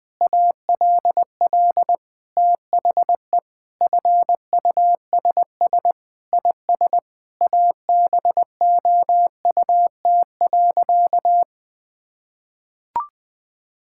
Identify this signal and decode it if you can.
Morse code, heard as a keyed tone, about 20 wpm, 700 Hz. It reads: ALL THE FUSS IS ABOUT.
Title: Sentences with 8-Word Phrases - 20wpm